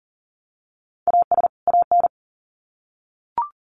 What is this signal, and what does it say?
Morse code, heard as a keyed tone, about 40 wpm, 700 Hz. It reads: AS AN
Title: Sets of 2 Words 40wpm